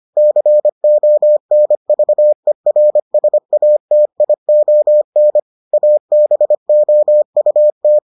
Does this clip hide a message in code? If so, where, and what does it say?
Yes: Morse code from 0.2 to 8.0 s, heard as a keyed tone, about 25 wpm, 600 Hz. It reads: CONVERSATION ABOUT